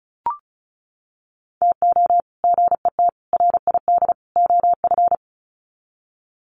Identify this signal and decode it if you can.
Morse code, heard as a keyed tone, about 34 wpm, 700 Hz. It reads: TO GET RID OF
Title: Sets of 4 Words 35wpm